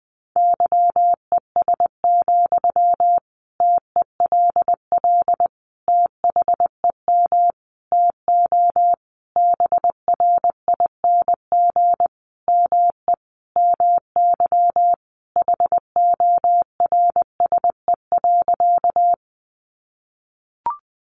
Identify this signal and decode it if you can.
Morse code, heard as a keyed tone, about 20 wpm, 700 Hz. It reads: YES, TELL THEM TO BRING ME MY HORSE.
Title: Sentences from Top 500 Words 20wpm